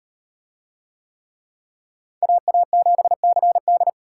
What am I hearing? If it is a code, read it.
Morse code, heard as a keyed tone, about 37 wpm, 700 Hz. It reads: AA7CD